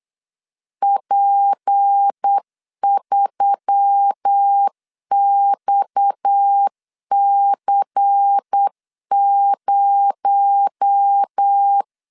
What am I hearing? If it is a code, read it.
Morse code, heard as a keyed tone, about 8 wpm, 800 Hz. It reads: P3XC0
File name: LEZIONI / 6 PKX0 / 6 PKX0 8wpm 3341